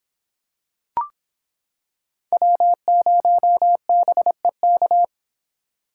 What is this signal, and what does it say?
Morse code, heard as a keyed tone, about 26 wpm, 700 Hz. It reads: W0BEK